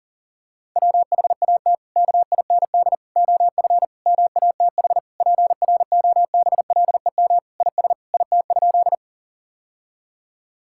Morse code, heard as a keyed tone, about 40 wpm, 700 Hz. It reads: WHAT KIND OF MATH PROBLEM IS IT?
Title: Sentences from Top 1000 Words - Review Part 6 - 40wpm